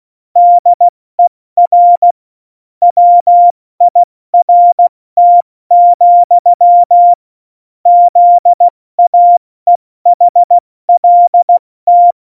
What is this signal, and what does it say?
Morse code, heard as a keyed tone, about 16 wpm, 700 Hz. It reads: DER WIRT, ZAEHLT
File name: Morsekurs / Witze / Witz 15. Oktober 2019, 70 F80